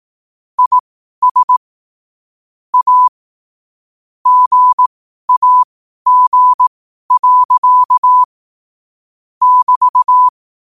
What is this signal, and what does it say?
Morse code, heard as a keyed tone, about 18 wpm, spaced as for 12 wpm, 1000 Hz. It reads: IS A GAG. <BT>
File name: qotdfiles / 12wpm